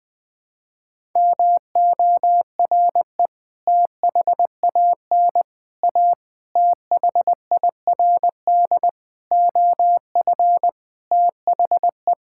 Morse code, heard as a keyed tone, about 20 wpm, 700 Hz. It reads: MORE THAN A THIRD OF THE
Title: Sentences with 6-Word Phrases - 20wpm